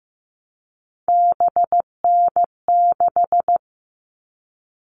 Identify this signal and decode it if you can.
Morse code, heard as a keyed tone, about 15 wpm, 700 Hz. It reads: BN6
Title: All Cut Numbers - 3-Digit 15wpm